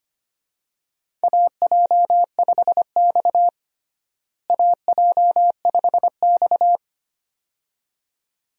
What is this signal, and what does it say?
Morse code, heard as a keyed tone, about 25 wpm, 700 Hz. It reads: AJ5X AJ5X
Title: USA Call Signs - 4 Characters - Repeated 2x - 25wpm